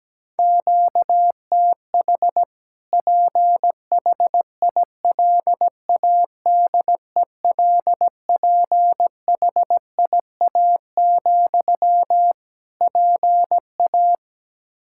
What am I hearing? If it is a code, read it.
Morse code, heard as a keyed tone, about 17 wpm, 700 Hz. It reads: QTH PHILADELPHIA, PA
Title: Practice QSO - Exchange 1 - Beginning - 17wpm